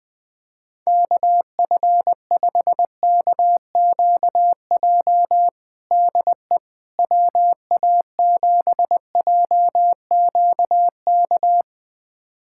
Morse code, heard as a keyed tone, about 20 wpm, 700 Hz. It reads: KF5KQJ DE WA7JQK
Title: USA Callsign DE Callsign 20wpm